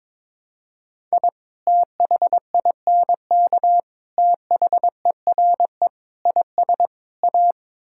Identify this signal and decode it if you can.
Morse code, heard as a keyed tone, about 22 wpm, 700 Hz. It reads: I THINK THERE IS A
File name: practice / 22wpm / Build A Phrase - 5-Word Phrases - 22wpm